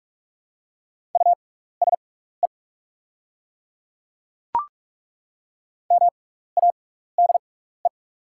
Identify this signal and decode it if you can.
Morse code, heard as a keyed tone, about 43 wpm, spaced as for 15 wpm, 700 Hz. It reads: USE MADE
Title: 6x Character Spacing - Top 200 Words - 45wpm